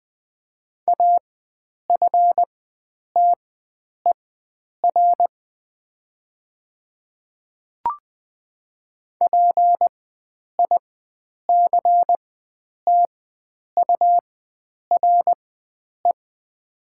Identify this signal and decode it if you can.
Morse code, heard as a keyed tone, about 20 wpm, spaced as for 9 wpm, 700 Hz. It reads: AFTER PICTURE
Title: 4x Character Spacing - Top 300 Words - 20wpm